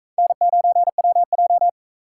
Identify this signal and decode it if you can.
Morse code, heard as a keyed tone, about 38 wpm, 700 Hz. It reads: N9WJ